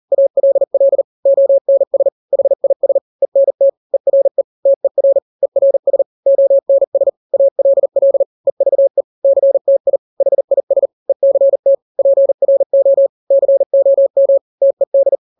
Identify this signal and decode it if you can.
Morse code, heard as a keyed tone, about 40 wpm, 550 Hz. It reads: ALL ONS HIS ENT ERE TER ERS ONS ALL EVE CTI HIS ECT PRO COM TED
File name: n-grams / 3-grams 40 35 WPM